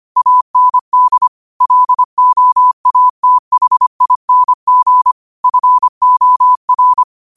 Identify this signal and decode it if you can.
Morse code, heard as a keyed tone, about 25 wpm, 1000 Hz. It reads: AND LOATHING FOR